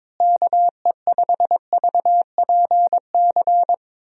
Morse code, heard as a keyed tone, about 22 wpm, 700 Hz. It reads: KE5VPC
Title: USA Call Signs - 6 Characters - Repeated 3x - 22wpm